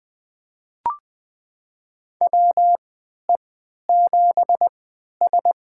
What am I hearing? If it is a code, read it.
Morse code, heard as a keyed tone, about 20 wpm, spaced as for 11 wpm, 700 Hz. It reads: WE7S
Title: USA Call Signs - 4 Characters - 3x Character Spacing - 20wpm